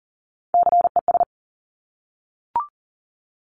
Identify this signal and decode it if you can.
Morse code, heard as a keyed tone, about 42 wpm, 700 Hz. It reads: CES
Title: Top Three Letter Combinations in Words 40wpm